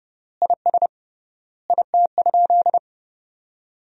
Morse code, heard as a keyed tone, about 32 wpm, 700 Hz. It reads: IS IT?
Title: 3x Word-Spacing - Sentences from Top 200 Words - 30wpm